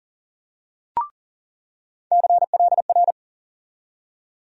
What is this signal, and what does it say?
Morse code, heard as a keyed tone, about 38 wpm, 700 Hz. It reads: CLR
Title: USA Call Sign Postfixes - 3 Letters 40wpm